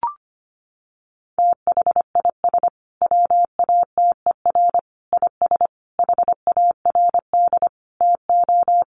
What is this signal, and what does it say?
Morse code, heard as a keyed tone, about 25 wpm, 700 Hz. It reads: THIS WATER IS HARD TO
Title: Sentences from Top 1300 Words 25wpm